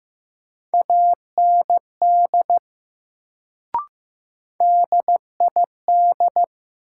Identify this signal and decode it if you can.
Morse code, heard as a keyed tone, about 15 wpm, 700 Hz. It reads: AND DID
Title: Sentences from Top 100 Words - New Word Drill - 15wpm